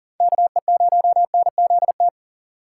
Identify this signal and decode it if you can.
Morse code, heard as a keyed tone, about 43 wpm, 700 Hz. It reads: KE0NZT
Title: USA Call Signs - 6 Characters - 6x Character Spacing - 40wpm